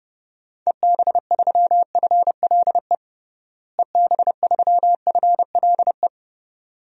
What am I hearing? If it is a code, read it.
Morse code, heard as a keyed tone, about 30 wpm, 700 Hz. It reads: EB3FLE EB3FLE
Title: DXCC-100 Call Signs - 6 Characters - Repeated 2x - 30wpm